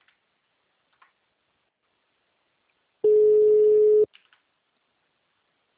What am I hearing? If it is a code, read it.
Ringback tone (Europe, 425 Hz).